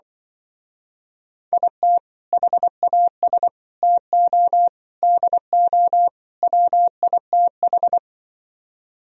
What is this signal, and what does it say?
Morse code, heard as a keyed tone, about 24 wpm, 700 Hz. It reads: IT HAS TO DO WITH
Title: Speed-Racing - Sets of 5 Words 20wpm